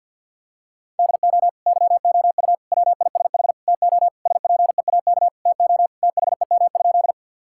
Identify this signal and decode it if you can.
Morse code, heard as a keyed tone, about 50 wpm, 700 Hz. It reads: DO YOU WISH TO SPEAK TO THEM?
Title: Sentences from Top 1600 Words 50wpm